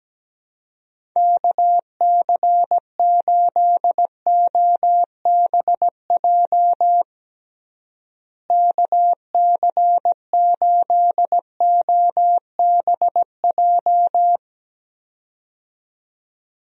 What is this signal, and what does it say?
Morse code, heard as a keyed tone, about 17 wpm, 700 Hz. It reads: KC8OBJ KC8OBJ
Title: USA Call Signs - 6 Characters - Repeated 2x - 17wpm